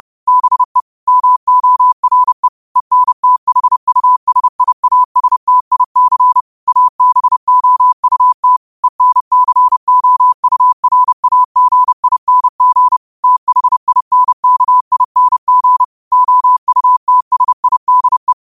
Morse code, heard as a keyed tone, about 30 wpm, 1000 Hz. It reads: BE MORE ENTHUSIASTIC ABOUT ENCOURAGING THINKING OUTSIDE
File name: qotdfiles / 30wpm